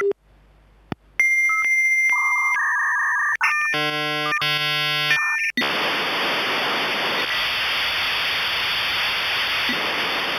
Dial-up modem handshake.